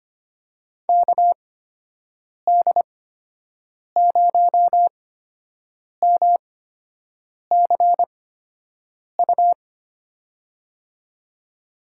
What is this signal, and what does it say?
Morse code, heard as a keyed tone, about 25 wpm, spaced as for 7 wpm, 700 Hz. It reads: KD0MCU
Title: USA Call Signs - 6 Characters - 8x Character Spacing - 25wpm